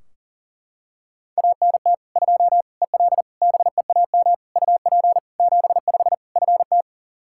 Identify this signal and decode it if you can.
Morse code, heard as a keyed tone, about 37 wpm, 700 Hz. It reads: ANT 2 EL BEAM UP 75 FT
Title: Practice QSO - Exchange 2 - Info - 40wpm